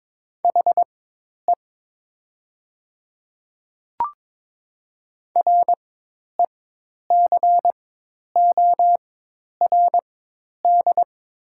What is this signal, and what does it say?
Morse code, heard as a keyed tone, about 22 wpm, spaced as for 10 wpm, 690 Hz. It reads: HE RECORD
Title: 4x Character Spacing - Top 500 Words - 22wpm